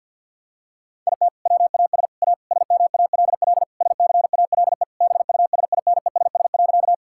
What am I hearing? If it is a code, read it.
Morse code, heard as a keyed tone, about 50 wpm, 700 Hz. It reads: IT WAS A SMALL SCALE BUSINESS.
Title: Sentences from Top 1000 Words - Review Part 5 - 50wpm